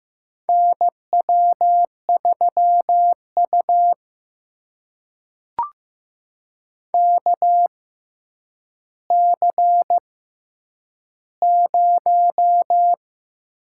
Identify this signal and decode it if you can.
Morse code, heard as a keyed tone, about 15 wpm, spaced as for 8 wpm, 700 Hz. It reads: NW3U KC0
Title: USA Call Signs - 6x Character Spacing - 15wpm